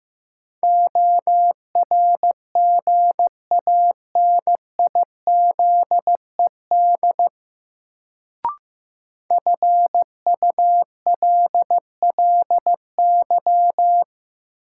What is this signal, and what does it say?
Morse code, heard as a keyed tone, about 15 wpm, 700 Hz. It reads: ORGANIZED FULLY
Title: Sentences from Top 1300 Words - New Word Drill - 15wpm